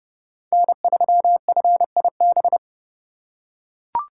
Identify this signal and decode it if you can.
Morse code, heard as a keyed tone, about 29 wpm, 700 Hz. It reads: N3FIB